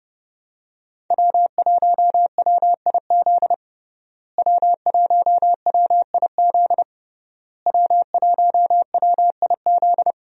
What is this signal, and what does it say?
Morse code, heard as a keyed tone, about 32 wpm, 700 Hz. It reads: W1WIZ W1WIZ W1WIZ
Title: USA Call Signs - 5 Characters - Repeated 3x - 30wpm